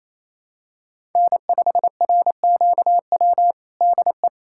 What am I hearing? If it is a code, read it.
Morse code, heard as a keyed tone, about 28 wpm, 700 Hz. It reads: N5RQW DE
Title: USA Callsign DE Callsign 28wpm